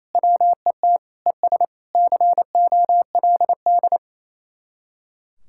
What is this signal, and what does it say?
Morse code, heard as a keyed tone, about 38 wpm, 700 Hz. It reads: WET ES COLD